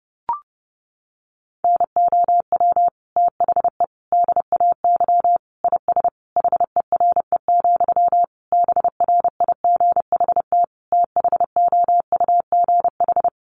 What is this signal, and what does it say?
Morse code, heard as a keyed tone, about 30 wpm, 700 Hz. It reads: NOW THE DAY IS HERE, BRIGHT THOUGH
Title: Sentences from Top 1000 Words 30wpm